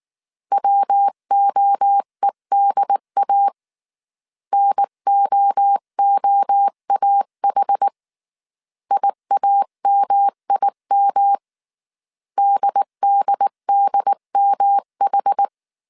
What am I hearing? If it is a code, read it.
Morse code, heard as a keyed tone, about 19 wpm, 800 Hz. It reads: WOEDA NOOAH IAMIM DDDMH